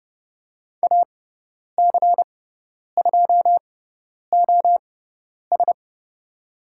Morse code, heard as a keyed tone, about 30 wpm, spaced as for 10 wpm, 700 Hz. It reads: AC2OS
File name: Lesson 16g - Letter c - Call Signs 30wpm